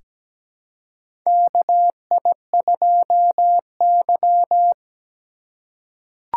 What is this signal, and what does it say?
Morse code, heard as a keyed tone, about 17 wpm, 700 Hz. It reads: KI2Y
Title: USA Call Signs - 4 Characters - 4x Character Spacing - 17wpm